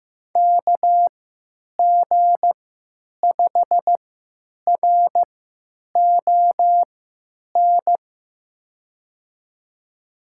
Morse code, heard as a keyed tone, about 15 wpm, spaced as for 9 wpm, 700 Hz. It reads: KG5RON
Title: USA Call Signs - 6 Characters - 3x Character Spacing - 15wpm